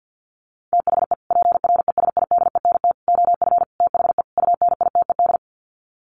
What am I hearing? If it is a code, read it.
Morse code, heard as a keyed tone, about 50 wpm, 700 Hz. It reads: THE PRESIDENT OF THE UNITED
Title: Sets of 5 Words 50wpm